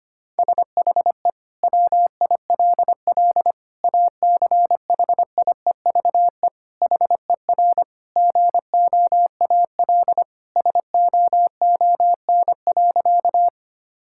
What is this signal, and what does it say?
Morse code, heard as a keyed tone, about 25 wpm, 700 Hz. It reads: SHE WILL ACHIEVE HER GOAL SOON.
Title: Sentences from Top 1000 Words - Review Part 6 - 25wpm